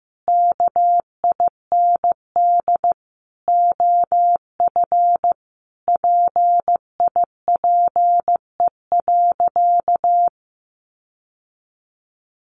Morse code, heard as a keyed tone, about 14 wpm, 700 Hz. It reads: KIND OF PIPE.
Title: Sentences from Top 1500 Words 15wpm